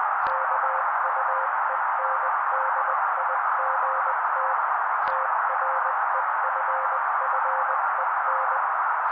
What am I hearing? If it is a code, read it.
Morse code, heard as a keyed tone, about 20 wpm, 530 Hz. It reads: KUENDIGT TREFFEN
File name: daily / 18wpm noise